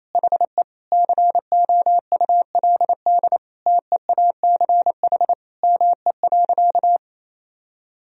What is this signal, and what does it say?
Morse code, heard as a keyed tone, about 28 wpm, 700 Hz. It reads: HE COULD TEACH ME.
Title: Sentences from Top 1000 Words - Review Part 2 - 28wpm